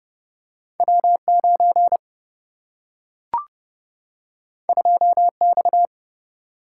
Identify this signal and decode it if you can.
Morse code, heard as a keyed tone, about 30 wpm, 700 Hz. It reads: W9 2X
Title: Single-Letter Single-Number 30wpm